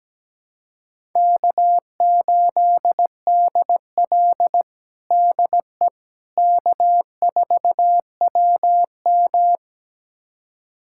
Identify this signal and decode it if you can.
Morse code, heard as a keyed tone, about 17 wpm, 700 Hz. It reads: K8DL DE K4WM
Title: USA Callsign DE Callsign - 4 Characters - 17wpm